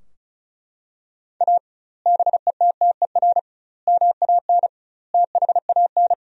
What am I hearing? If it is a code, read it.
Morse code, heard as a keyed tone, about 36 wpm, 700 Hz. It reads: A BETTER MAN THAN
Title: 2x Word-Spacing - Sentences from Top 400 Words - 35wpm